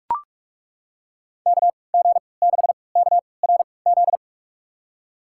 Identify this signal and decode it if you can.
Morse code, heard as a keyed tone, about 42 wpm, spaced as for 25 wpm, 700 Hz. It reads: KG6KRZ
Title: USA Call Signs - 6 Characters - 3x Character Spacing - 45wpm